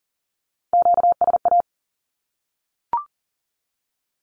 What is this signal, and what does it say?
Morse code, heard as a keyed tone, about 40 wpm, 700 Hz. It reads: QSA
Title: Q-Codes 40wpm